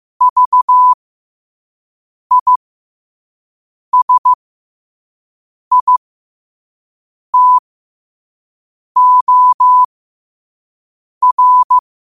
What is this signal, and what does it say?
Morse code, heard as a keyed tone, about 15 wpm, spaced as for 5 wpm, 1000 Hz. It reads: VISITOR
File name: qotdfiles / 5wpm